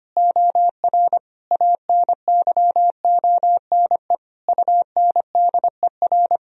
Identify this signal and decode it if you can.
Morse code, heard as a keyed tone, about 25 wpm, 700 Hz. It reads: OR ANYONE UNDER
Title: Sentences from Top 1000 Words - Review Part 1 - 25wpm